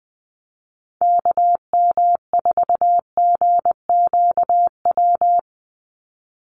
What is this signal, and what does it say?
Morse code, heard as a keyed tone, about 20 wpm, 700 Hz. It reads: KM4GQW
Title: USA Call Signs 20wpm